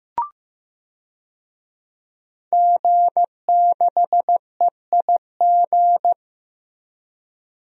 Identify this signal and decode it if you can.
Morse code, heard as a keyed tone, about 15 wpm, 700 Hz. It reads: G6EIG
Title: DXCC-100 Call Signs - 5 Characters - Repeated 3x - 15wpm